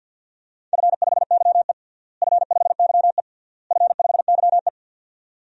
Morse code, heard as a keyed tone, about 50 wpm, 700 Hz. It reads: F5YE F5YE F5YE
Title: DXCC-100 Call Signs - 4 Characters - Repeated 3x - 50wpm